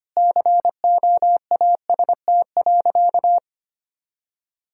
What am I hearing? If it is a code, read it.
Morse code, heard as a keyed tone, about 25 wpm, 700 Hz. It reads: COAST.
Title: Sentences from Top 1000 Words - Review Part 4 - 25wpm